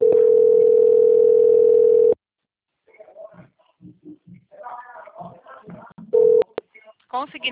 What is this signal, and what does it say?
Ringback tone (North America), 440 + 480 Hz, with speech.
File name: out- - - - - .55453